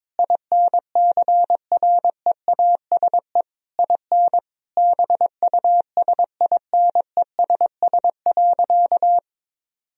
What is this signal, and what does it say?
Morse code, heard as a keyed tone, about 22 wpm, 700 Hz. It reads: INCREASE IN BUSINESS.
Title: Sentences from Top 1000 Words - Review Part 4 - 22wpm